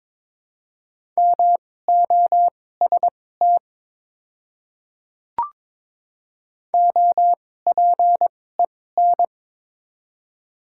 Morse code, heard as a keyed tone, about 22 wpm, spaced as for 16 wpm, 700 Hz. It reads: MOST OPEN
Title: 2x Character Spacing - Top 300 Words - 22wpm